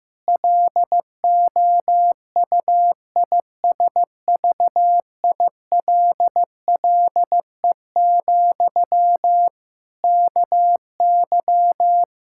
Morse code, heard as a keyed tone, about 15 wpm, 700 Hz. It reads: LOUISVILLE, KY
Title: K1USN SST QSO Practice 15wpm